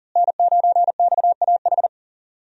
Morse code, heard as a keyed tone, about 40 wpm, 700 Hz. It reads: N9XAH
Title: USA Call Signs - 5 Characters - Repeated 2x - 40wpm